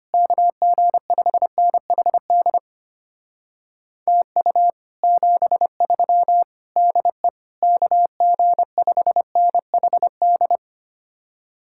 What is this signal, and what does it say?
Morse code, heard as a keyed tone, about 33 wpm, 700 Hz. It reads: KG5NHD TU 73 DE KG5NHD